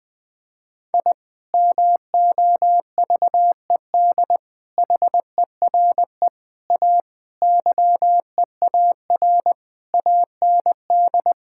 Morse code, heard as a keyed tone, about 21 wpm, 700 Hz. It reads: I MOVED HERE A YEAR AND
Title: Sentences with 6-Word Phrases - 20wpm